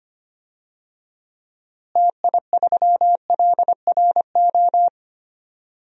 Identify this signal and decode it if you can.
Morse code, heard as a keyed tone, about 25 wpm, 700 Hz. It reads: TI3LRO
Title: DXCC-100 Call Signs - 6 Characters - Repeated 2x - 25wpm